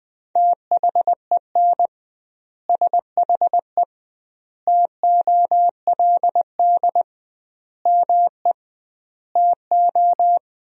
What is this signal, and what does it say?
Morse code, heard as a keyed tone, about 21 wpm, 700 Hz. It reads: THEN SHE TOLD ME TO
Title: 2x Word-Spacing - Sentences from Top 400 Words - 20wpm